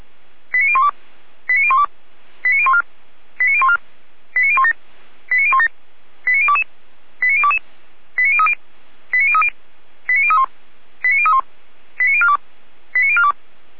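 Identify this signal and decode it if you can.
Five-tone selective call (ZVEI), heard as quick runs of five tones, tones of about 70 ms. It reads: 89012 89012 89014 89014 89017 89017 89022 89022 89030 89030 89031 89031 89042 89042
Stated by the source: ZVEI five-tone call 89020 received by SDR.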